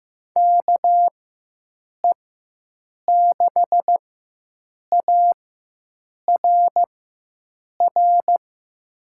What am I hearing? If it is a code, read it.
Morse code, heard as a keyed tone, about 15 wpm, spaced as for 7 wpm, 700 Hz. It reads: KE6ARR
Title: USA Call Signs - 6 Characters - 4x Character Spacing - 15wpm